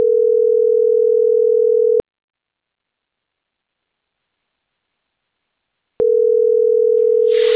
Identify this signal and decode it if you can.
Ringback tone (North America), 440 + 480 Hz.